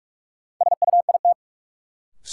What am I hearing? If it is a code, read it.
Morse code, heard as a keyed tone, about 43 wpm, 700 Hz. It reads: SUIT
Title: Sentences from Top 1600 Words - New Word Drill - 45wpm